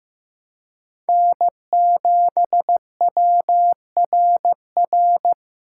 Morse code, heard as a keyed tone, about 15 wpm, 700 Hz. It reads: N7WRR
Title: USA Call Signs - 5 Characters - 2x Character Spacing - 15wpm